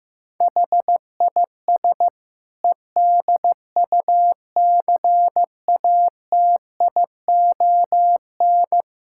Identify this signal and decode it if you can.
Morse code, heard as a keyed tone, about 15 wpm, 700 Hz. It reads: HIS EDUCATION
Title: Sentences from Top 1000 Words - Review Part 6 - 15wpm